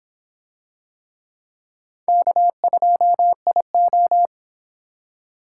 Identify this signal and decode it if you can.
Morse code, heard as a keyed tone, about 26 wpm, 700 Hz. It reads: K2IO